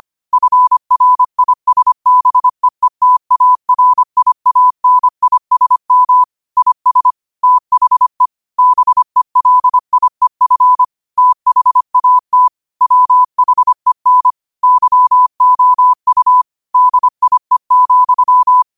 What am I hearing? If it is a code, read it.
Morse code, heard as a keyed tone, about 25 wpm, 1000 Hz. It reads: FRISBEETARIANISM IS THE BELIEF THAT WHEN YOU DIE,
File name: qotdfiles / 25wpm